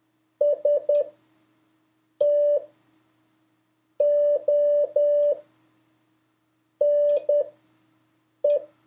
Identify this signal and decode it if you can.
Morse code, heard as a keyed tone, about 10 wpm, spaced as for 5 wpm, 580 Hz. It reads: STONE
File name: 4of5-words-3